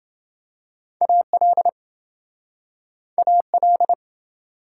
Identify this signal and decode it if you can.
Morse code, heard as a keyed tone, about 28 wpm, 700 Hz. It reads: AL AL